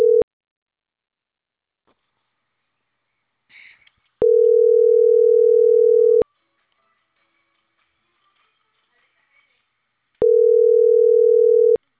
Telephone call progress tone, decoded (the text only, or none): ringback tone